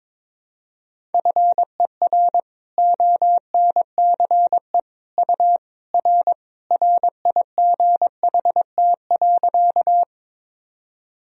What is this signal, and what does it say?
Morse code, heard as a keyed tone, about 21 wpm, 700 Hz. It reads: FER ONCE U R RIGHT.
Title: Sentences from Top 300 Words and Abbreviations 22wpm